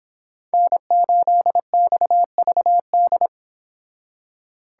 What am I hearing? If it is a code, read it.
Morse code, heard as a keyed tone, about 31 wpm, 700 Hz. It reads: N8XVD